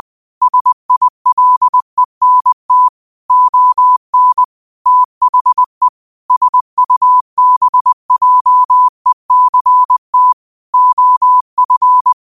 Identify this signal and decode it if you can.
Morse code, heard as a keyed tone, about 20 wpm, 1000 Hz. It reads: SILENT ON THE SUBJECT OF
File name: qotdfiles / 20wpm